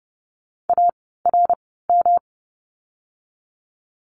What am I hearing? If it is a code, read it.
Morse code, heard as a keyed tone, about 30 wpm, spaced as for 17 wpm, 700 Hz. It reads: ARM
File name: course / 30:10wpm / 108 Lesson 18f - Letter m - Words 30wpm